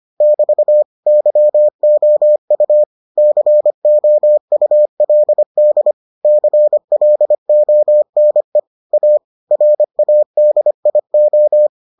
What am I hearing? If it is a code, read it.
Morse code, heard as a keyed tone, about 25 wpm, 600 Hz. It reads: <BT> YOU COULD CLONE A RADIO